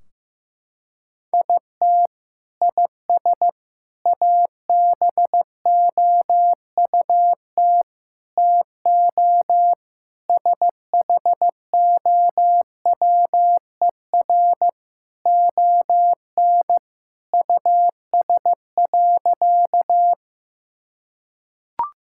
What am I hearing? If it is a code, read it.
Morse code, heard as a keyed tone, about 15 wpm, 700 Hz. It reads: IT IS ABOUT TO SHOWER ON US.
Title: Sentences from Top 1600 Words 15wpm